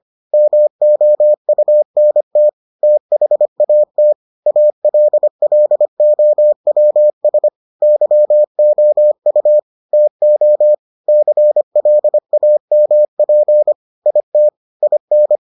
Morse code, heard as a keyed tone, about 25 wpm, 600 Hz. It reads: MOUNT THAT ALLOWS YOU TO CLAMP IT IN